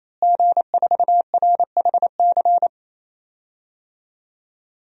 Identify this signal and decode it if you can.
Morse code, heard as a keyed tone, about 28 wpm, 700 Hz. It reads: G4RHC
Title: DXCC-100 Call Signs - 5 Characters - Repeated 3x - 28wpm